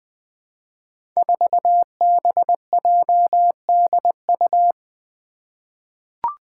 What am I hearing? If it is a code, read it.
Morse code, heard as a keyed tone, about 20 wpm, 700 Hz. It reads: 4BJDU
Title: Five Challenging Letter-Number Combinations 20wpm